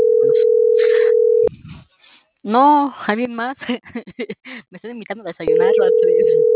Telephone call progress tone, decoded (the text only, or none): ringback tone